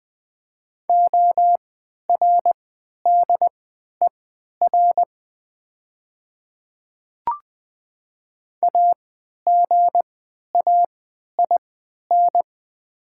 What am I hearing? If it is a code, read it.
Morse code, heard as a keyed tone, about 20 wpm, spaced as for 11 wpm, 700 Hz. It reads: ORDER AGAIN
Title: 3x Character Spacing - Top 500 Words - 20wpm